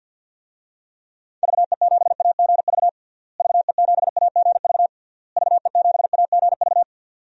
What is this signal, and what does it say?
Morse code, heard as a keyed tone, about 51 wpm, 700 Hz. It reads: VE7AGV VE7AGV VE7AGV
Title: DXCC-100 Call Signs - 6 Characters - Repeated 3x - 50wpm